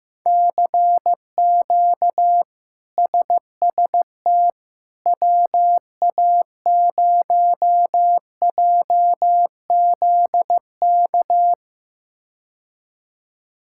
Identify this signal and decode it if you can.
Morse code, heard as a keyed tone, about 16 wpm, 700 Hz. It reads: CQ SST WA0JZK
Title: K1USN SST QSO Practice - 15wpm